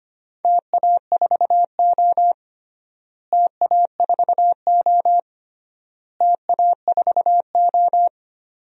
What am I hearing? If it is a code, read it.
Morse code, heard as a keyed tone, about 25 wpm, 700 Hz. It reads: TA4O TA4O TA4O
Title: DXCC-100 Call Signs - 4 Characters - Repeated 3x - 25wpm